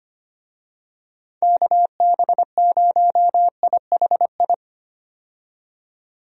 Morse code, heard as a keyed tone, about 25 wpm, 700 Hz. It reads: KB0IHI
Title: USA Call Signs - 6 Characters - 25wpm